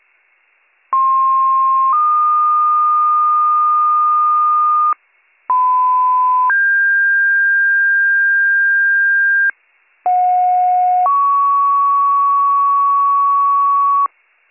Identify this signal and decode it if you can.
Two-tone page, heard as a one-second tone, then a three-second tone, about 1060 then 1220 Hz, about 1010 then 1600 Hz, about 730 then 1120 Hz. It is heard from 0.9 to 14.1 s.